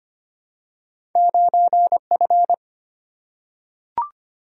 Morse code, heard as a keyed tone, about 25 wpm, 690 Hz. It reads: 9F
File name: Two Letter-Number Combinations - 25wpm